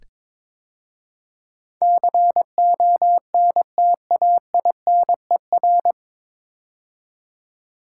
Morse code, heard as a keyed tone, about 22 wpm, 700 Hz. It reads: CONTAINER
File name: 9 Letter Words - First 3 Letters Given - 22wpm